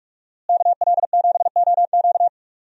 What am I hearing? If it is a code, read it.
Morse code, heard as a keyed tone, about 46 wpm, 700 Hz. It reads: KL7OQ
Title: DXCC-100 Call Signs - 5 Characters - Repeated 3x - 45wpm